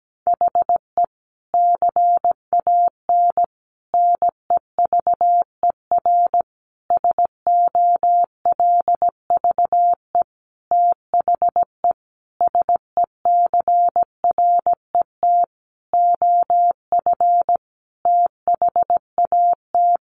Morse code, heard as a keyed tone, about 17 wpm, 700 Hz. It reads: HE CAN NEVER SOLVE THE SECRET OF THAT
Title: Sentences from Top 1300 Words 17wpm